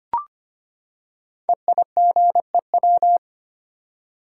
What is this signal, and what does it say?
Morse code, heard as a keyed tone, about 25 wpm, 700 Hz. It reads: EIGEW
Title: Code Groups - 5 Letters - 25wpm